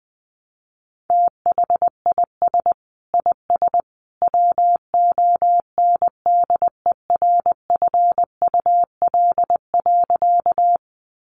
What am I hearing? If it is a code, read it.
Morse code, heard as a keyed tone, about 20 wpm, 700 Hz. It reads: THIS IS WONDERFUL.
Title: Sentences from Top 1100 Words 20wpm